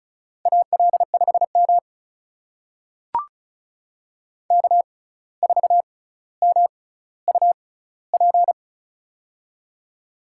Morse code, heard as a keyed tone, about 36 wpm, spaced as for 17 wpm, 700 Hz. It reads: AL5M K4MUP
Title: USA Call Signs - 6x Character Spacing - 35wpm